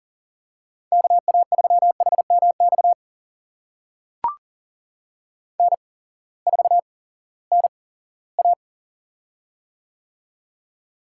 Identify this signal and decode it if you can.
Morse code, heard as a keyed tone, about 40 wpm, spaced as for 20 wpm, 700 Hz. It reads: KA3HMX N4NA
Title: USA Call Signs - 8x Character Spacing - 40wpm